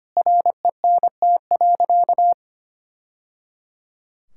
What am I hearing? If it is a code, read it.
Morse code, heard as a keyed tone, about 25 wpm, 700 Hz. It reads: RENT.
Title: Sentences from Top 1000 Words - Review Part 5 - 25wpm